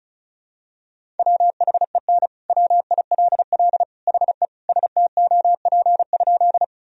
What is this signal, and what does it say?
Morse code, heard as a keyed tone, about 36 wpm, 700 Hz. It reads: WHEN WILL HE STOP?
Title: Sentences from Top 1000 Words - Review Part 1 - 35wpm